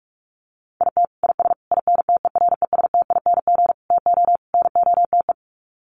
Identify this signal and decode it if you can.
Morse code, heard as a keyed tone, about 44 wpm, 700 Hz. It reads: IT IS INTERESTING TO NOTE
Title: Sets of 5 Words Encore2 45wpm